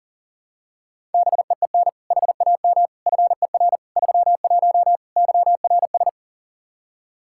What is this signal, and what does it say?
Morse code, heard as a keyed tone, about 40 wpm, 700 Hz. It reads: BEEN HAM FER 31 YRS
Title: Practice QSO - Exchange 3 - More Info - 40wpm